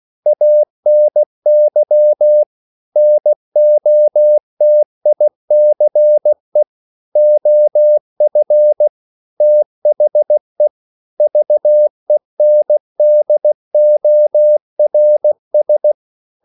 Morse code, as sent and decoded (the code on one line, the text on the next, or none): .- -. -.-- / -. --- - .. -.-. / --- ..-. / - .... / ...- . -. -.. --- .-. ...
ANY NOTICE OF THE VENDORS